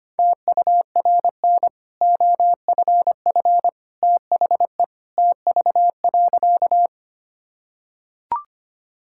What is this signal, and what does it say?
Morse code, heard as a keyed tone, about 25 wpm, 700 Hz. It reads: TURN OFF THE TV.
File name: Easy Sentences with Hints - 25wpm